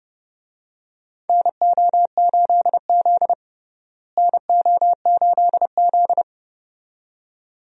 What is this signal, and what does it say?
Morse code, heard as a keyed tone, about 32 wpm, 700 Hz. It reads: NO8Z NO8Z
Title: USA Call Signs - 4 Characters - Repeated 2x - 30wpm